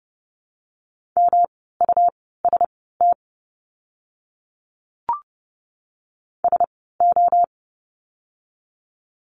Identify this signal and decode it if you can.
Morse code, heard as a keyed tone, about 30 wpm, spaced as for 17 wpm, 700 Hz. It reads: MUST SO
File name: course / 30:10wpm / 237 Lesson 38f - Letter q - Words 30wpm